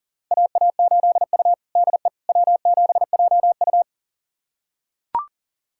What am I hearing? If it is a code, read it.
Morse code, heard as a keyed tone, about 40 wpm, 700 Hz. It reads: AA8U DE W7JU
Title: USA Callsign DE Callsign 40wpm